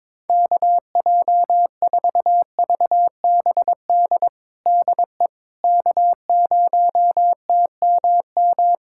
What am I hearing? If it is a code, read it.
Morse code, heard as a keyed tone, about 22 wpm, 700 Hz. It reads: KJ4VBD DE K0TMM
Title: USA Callsign DE Callsign 22wpm